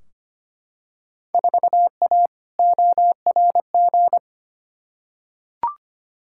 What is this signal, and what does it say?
Morse code, heard as a keyed tone, about 25 wpm, 700 Hz. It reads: 4A ORG
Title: Field Day Exchanges - 25wpm